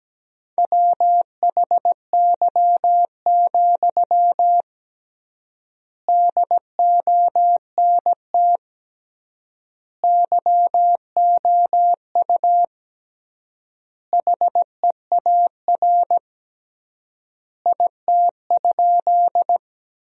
Morse code, heard as a keyed tone, about 17 wpm, 700 Hz. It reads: WHY, DONT YOU HEAR IT?
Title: 3x Word-Spacing - Sentences from Top 300 Words - 17wpm